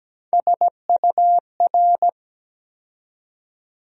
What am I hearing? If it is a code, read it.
Morse code, heard as a keyed tone, about 17 wpm, 700 Hz. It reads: SUR
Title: USA Call Sign Postfixes - 3 Letters 17wpm